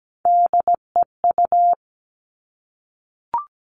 Morse code, heard as a keyed tone, about 17 wpm, 700 Hz. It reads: DEU
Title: All Cut Numbers - 3-Digit 17wpm